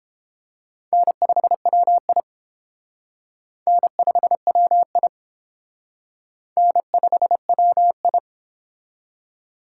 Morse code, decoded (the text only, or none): N5WI N5WI N5WI